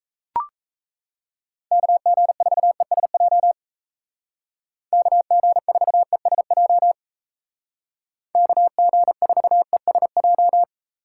Morse code, heard as a keyed tone, about 33 wpm, 700 Hz. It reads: KG4ESJ KG4ESJ KG4ESJ
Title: Speed Racing - USA Call Signs 28wpm